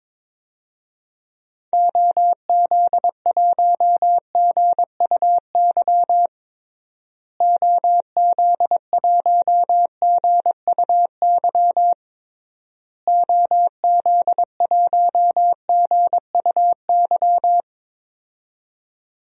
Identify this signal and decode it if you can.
Morse code, heard as a keyed tone, about 22 wpm, 700 Hz. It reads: OZ1GUY OZ1GUY OZ1GUY
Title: DXCC-100 Call Signs - 6 Characters - Repeated 3x - 22wpm